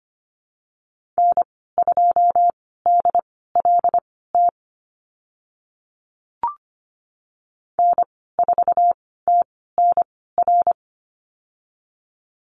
Morse code, heard as a keyed tone, about 25 wpm, spaced as for 16 wpm, 690 Hz. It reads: N2DLT N4TNR
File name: Lesson 15g - Number 5 - Call Signs 25wpm